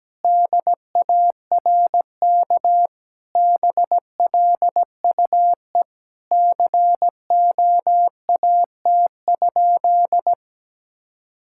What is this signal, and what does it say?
Morse code, heard as a keyed tone, about 17 wpm, 700 Hz. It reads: DARK BLUE COAT?
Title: Sentences from Top 1600 Words 17wpm